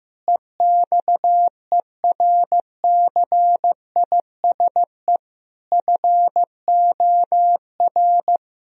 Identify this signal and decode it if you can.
Morse code, heard as a keyed tone, about 15 wpm, 700 Hz. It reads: EXERCISE FOR
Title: Sentences from Top 1000 Words - Review Part 4 - 15wpm